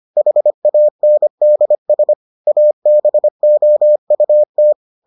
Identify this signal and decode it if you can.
Morse code, heard as a keyed tone, about 25 wpm, 600 Hz. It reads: HANDS ABOUT